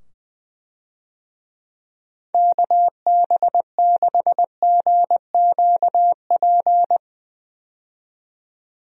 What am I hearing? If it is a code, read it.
Morse code, heard as a keyed tone, about 21 wpm, 700 Hz. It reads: KB6GQP
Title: USA Call Signs - 6 Characters - 20wpm